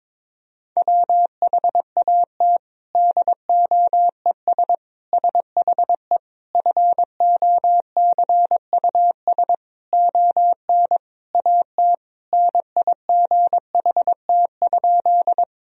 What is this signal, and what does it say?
Morse code, heard as a keyed tone, about 22 wpm, 700 Hz. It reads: WHAT DOES SHE FOCUS ON AT NIGHT?
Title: Sentences from Top 1000 Words - Review Part 4 - 22wpm